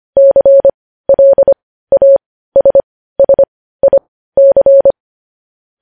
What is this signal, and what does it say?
Morse code, heard as a keyed tone, about 25 wpm, spaced as for 15 wpm, 550 Hz. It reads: CLASSIC
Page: news-headlines-in-morse-code-at-15-wpm